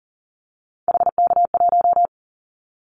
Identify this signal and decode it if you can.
Morse code, heard as a keyed tone, about 37 wpm, 700 Hz. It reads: HK1
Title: Callsign Prefixes for Top 100 DXCC - 40wpm